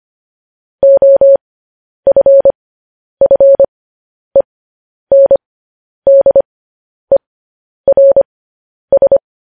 Morse code, heard as a keyed tone, about 25 wpm, spaced as for 10 wpm, 550 Hz. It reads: OFFENDERS
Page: news headlines in morse code at 10 wpm